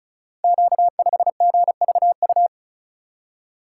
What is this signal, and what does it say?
Morse code, heard as a keyed tone, about 36 wpm, 700 Hz. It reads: Q5GVU
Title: Code Groups - 5 Letter:Numbers - 35wpm